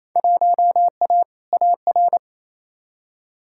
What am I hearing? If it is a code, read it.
Morse code, heard as a keyed tone, about 28 wpm, 700 Hz. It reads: 1A AR
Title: Field Day Exchanges Encore - 28wpm